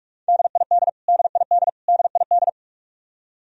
Morse code, heard as a keyed tone, about 46 wpm, 700 Hz. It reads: DID DID DID
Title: Top 100 Words Repeated 45wpm